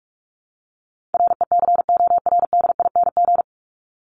Morse code, heard as a keyed tone, about 44 wpm, 700 Hz. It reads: RECORDING
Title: Top 1501-1750 Words 45wpm